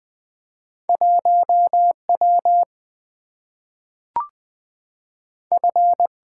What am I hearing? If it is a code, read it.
Morse code, heard as a keyed tone, about 20 wpm, 700 Hz. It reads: 1W F
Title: Single-Letter Single-Number 20wpm